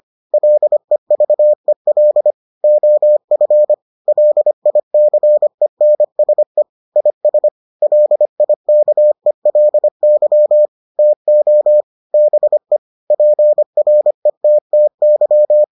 Morse code, heard as a keyed tone, about 25 wpm, 600 Hz. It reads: LEVEL OF LICENSE IS LIKELY TO BE PRETTY